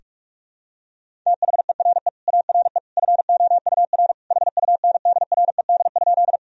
Morse code, heard as a keyed tone, about 41 wpm, 700 Hz. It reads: THERE ARE FOUR HUNDRED?
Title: Sentences from Top 1000 Words - Review Part 2 - 45wpm